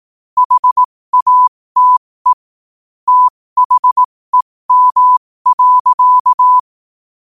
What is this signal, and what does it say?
Morse code, heard as a keyed tone, about 18 wpm, spaced as for 15 wpm, 1000 Hz. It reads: HATE THEM.
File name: qotdfiles / 15wpm